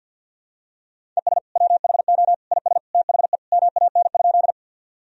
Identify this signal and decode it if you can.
Morse code, heard as a keyed tone, about 48 wpm, 700 Hz. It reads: ES WHO IS THE MAN?
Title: Sentences from Top 200 Words and Abbreviations 50wpm